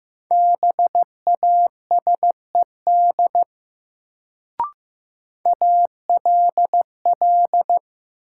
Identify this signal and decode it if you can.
Morse code, heard as a keyed tone, about 15 wpm, 700 Hz. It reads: BASED ALL
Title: Sets of 2 Words - New Word Drill - 15wpm